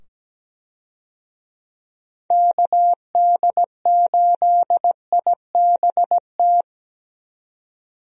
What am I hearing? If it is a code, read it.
Morse code, heard as a keyed tone, about 17 wpm, 700 Hz. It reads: KD8IBT
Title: USA Call Signs - 6 Characters - 17wpm